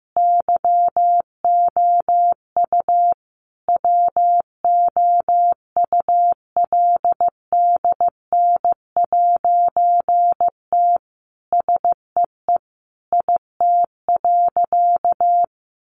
Morse code, heard as a keyed tone, about 15 wpm, 700 Hz. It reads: YOU WOULDN'T SEE IT.
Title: A Sound Of Thunder 15wpm